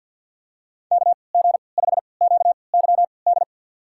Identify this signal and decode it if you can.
Morse code, heard as a keyed tone, about 53 wpm, spaced as for 29 wpm, 700 Hz. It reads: KG5QYD
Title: USA Call Signs - 6 Characters - 3x Character Spacing - 50wpm